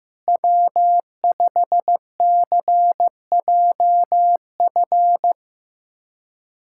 Morse code, heard as a keyed tone, about 16 wpm, 700 Hz. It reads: W5CJF
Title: USA Callsign DE Callsign - 5 Characters - 15wpm